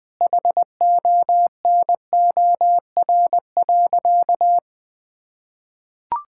Morse code, heard as a keyed tone, about 21 wpm, 700 Hz. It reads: HONOR.
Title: Sentences with 6-Word Phrases - Spoken Beginning - 20wpm